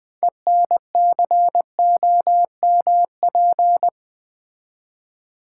Morse code, heard as a keyed tone, about 20 wpm, 700 Hz. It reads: ENCOMP
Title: Build A Word - 3 Letters At A Time - 9-Letter Words - 20wpm